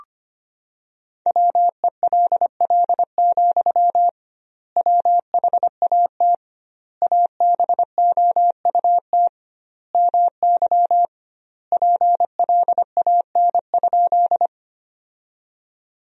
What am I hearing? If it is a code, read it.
Morse code, heard as a keyed tone, about 25 wpm, 700 Hz. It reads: WELL, WHAT ABOUT MY PLAN?
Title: 2x Word-Spacing - Sentences from Top 500 Words - 25wpm